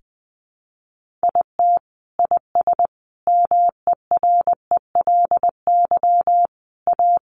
Morse code, heard as a keyed tone, about 20 wpm, 700 Hz. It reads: IT IS MERELY A
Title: Sentences from Top 1400 Words 20wpm